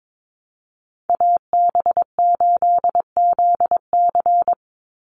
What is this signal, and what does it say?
Morse code, heard as a keyed tone, about 22 wpm, 700 Hz. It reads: AB8ZC
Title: USA Call Signs 22wpm